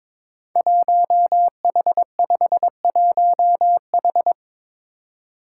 Morse code, heard as a keyed tone, about 22 wpm, 700 Hz. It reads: 1H51H
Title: Five Challenging Letter-Number Combinations 22wpm